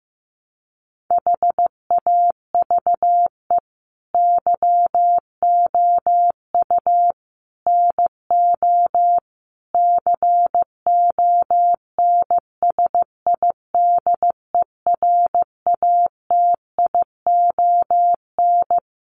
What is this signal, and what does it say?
Morse code, heard as a keyed tone, about 15 wpm, 700 Hz. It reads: HAVE YOU NO CONSIDERATION
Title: Sentences from Top 1500 Words 15wpm